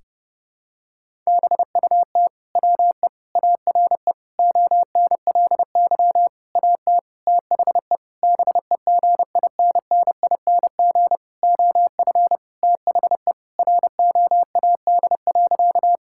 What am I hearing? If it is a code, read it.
Morse code, heard as a keyed tone, about 30 wpm, 700 Hz. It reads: BUT WE ARE ONLY AT THE BEGINNING OF THE ROAD.
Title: Sentences from Top 1000 Words - Review Part 5 - 30wpm